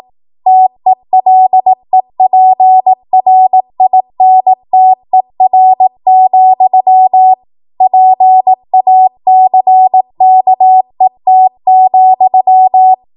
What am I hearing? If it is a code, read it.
Morse code, heard as a keyed tone, about 18 wpm, 750 Hz. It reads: TELEPRINTER, PACKET,
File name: Morse / 18WPM